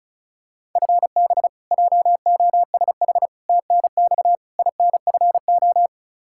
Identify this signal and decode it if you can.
Morse code, heard as a keyed tone, about 36 wpm, 700 Hz. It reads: FB JOSH TNX INFO
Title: Practice QSO - Exchange 3 - More Info - 35wpm